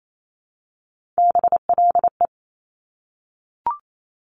Morse code, heard as a keyed tone, about 28 wpm, 700 Hz. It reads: BLE
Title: Top Three Letter Combinations in Words 28wpm